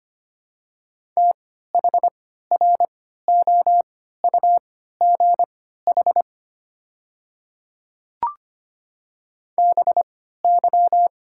Morse code, heard as a keyed tone, about 25 wpm, spaced as for 14 wpm, 700 Hz. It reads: THROUGH BY
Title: 3x Character Spacing - Top 300 Words - 25wpm